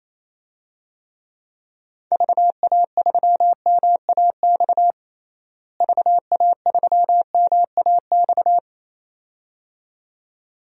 Morse code, heard as a keyed tone, about 27 wpm, 700 Hz. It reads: VA3MAX VA3MAX
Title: DXCC-100 Call Signs - 6 Characters - Repeated 2x - 28wpm